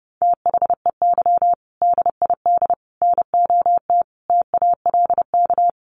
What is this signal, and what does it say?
Morse code, heard as a keyed tone, about 28 wpm, 700 Hz. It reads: THEY DID NOT TALK
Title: Sentences from Top 1200 Words 30wpm